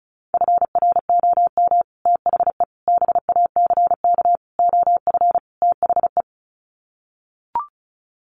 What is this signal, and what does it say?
Morse code, heard as a keyed tone, about 34 wpm, 700 Hz. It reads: FROM THE BACK OF THE
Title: Sets of 5 Words Encore 35wpm